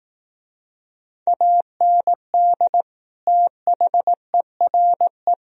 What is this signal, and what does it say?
Morse code, heard as a keyed tone, about 19 wpm, 700 Hz. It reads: AND THERE
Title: Speed-Racing - Sets of 2 Words 15wpm